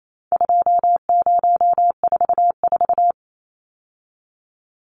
Morse code, heard as a keyed tone, about 28 wpm, 700 Hz. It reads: 2044
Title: Numbers - 4-Digit 28wpm